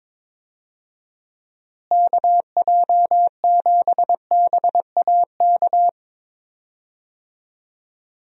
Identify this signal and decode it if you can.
Morse code, heard as a keyed tone, about 22 wpm, 700 Hz. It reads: KJ7BAK
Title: USA Call Signs - 6 Characters - Repeated 2x - 22wpm